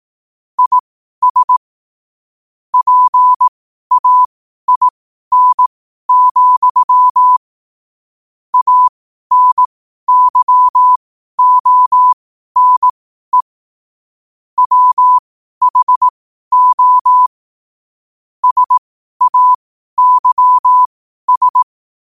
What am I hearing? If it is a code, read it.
Morse code, heard as a keyed tone, about 18 wpm, spaced as for 12 wpm, 1000 Hz. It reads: IS PAIN, ANYONE WHO SAYS
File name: qotdfiles / 12wpm